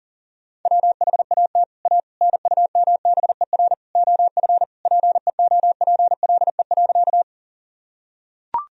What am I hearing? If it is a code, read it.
Morse code, heard as a keyed tone, about 40 wpm, 700 Hz. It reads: WHAT A NUMBER OF PEOPLE.
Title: Sentences from Top 1000 Words - Review Part 1 - 40wpm